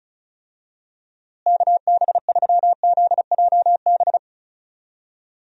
Morse code, heard as a keyed tone, about 34 wpm, 700 Hz. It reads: KB3ZJB
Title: USA Call Signs - 6 Characters - Repeated 3x - 35wpm